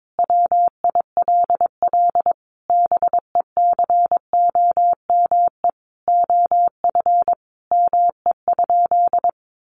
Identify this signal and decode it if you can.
Morse code, heard as a keyed tone, about 22 wpm, 700 Hz. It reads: WILL BECOME OF ME?
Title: Sentences from Top 600 Words 22wpm